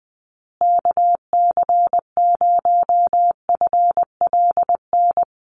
Morse code, heard as a keyed tone, about 20 wpm, 700 Hz. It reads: KC0FLN
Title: USA Call Signs Repeated 3x 20wpm